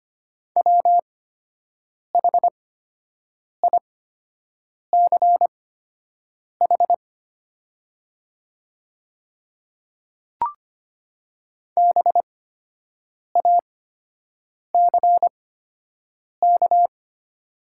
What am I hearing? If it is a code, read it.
Morse code, heard as a keyed tone, about 25 wpm, spaced as for 7 wpm, 700 Hz. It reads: WHICH BACK
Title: 8x Character Spacing - Top 300 Words - 25wpm